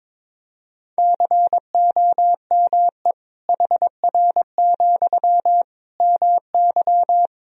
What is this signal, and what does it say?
Morse code, heard as a keyed tone, about 22 wpm, 700 Hz. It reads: COME HR, MY
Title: Sentences from Top 300 Words and Abbreviations 22wpm